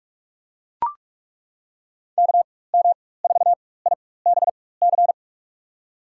Morse code, heard as a keyed tone, about 42 wpm, spaced as for 20 wpm, 700 Hz. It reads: KM4IBC
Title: USA Call Signs - 6 Characters - 4x Character Spacing - 45wpm